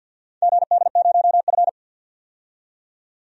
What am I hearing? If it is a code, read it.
Morse code, heard as a keyed tone, about 50 wpm, 700 Hz. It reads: GD0F
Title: DXCC-100 Call Signs - 4 Characters - 50wpm